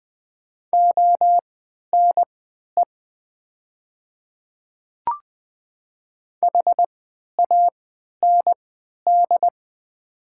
Morse code, heard as a keyed tone, about 20 wpm, spaced as for 11 wpm, 700 Hz. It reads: ONE HAND
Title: 3x Character Spacing - Top 500 Words - 20wpm